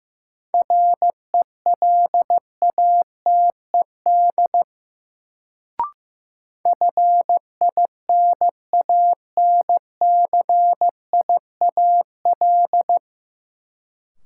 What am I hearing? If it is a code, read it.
Morse code, heard as a keyed tone, about 15 wpm, 700 Hz. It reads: RELATED FINANCIAL
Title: Sentences from Top 1000 Words - Review Part 5 - New Word Drill - 15wpm